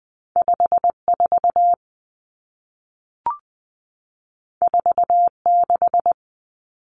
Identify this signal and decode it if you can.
Morse code, heard as a keyed tone, about 20 wpm, 700 Hz. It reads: 54 46
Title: Common Cut Numbers - 2-Digit 20wpm